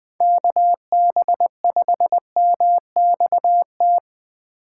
Morse code, heard as a keyed tone, about 24 wpm, 700 Hz. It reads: KB5MXT
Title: Speed Racing - USA Call Signs 20wpm